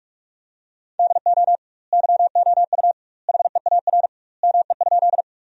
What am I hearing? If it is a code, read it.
Morse code, heard as a keyed tone, about 46 wpm, 700 Hz. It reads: DO YOU HEAR ME?
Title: 2x Word-Spacing - Sentences from Top 300 Words - 45wpm